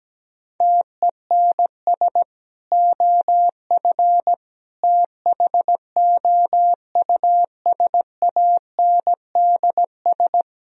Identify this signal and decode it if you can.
Morse code, heard as a keyed tone, about 17 wpm, 700 Hz. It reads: TENS OF THOUSANDS
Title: Speed-Racing - Sets of 4 Words 17wpm